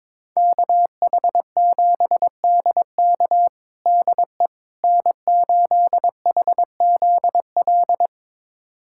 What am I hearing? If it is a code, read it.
Morse code, heard as a keyed tone, about 22 wpm, 700 Hz. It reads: KH7DK DE N8HZL
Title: USA Callsign DE Callsign - 5 Characters - 22wpm